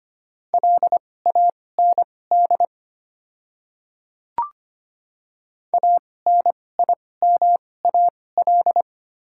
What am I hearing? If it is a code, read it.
Morse code, heard as a keyed tone, about 25 wpm, spaced as for 18 wpm, 700 Hz. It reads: LAND ANIMAL
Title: 2x Character Spacing - Top 300 Words - 25wpm